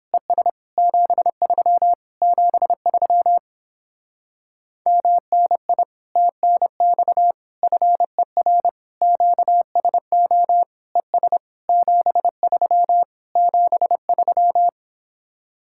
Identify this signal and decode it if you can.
Morse code, heard as a keyed tone, about 26 wpm, 700 Hz. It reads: ES 73 73 MNI TNX FER QSO ES 73 73
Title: Speed-Racing Common QSO Phrases Encore - 22wpm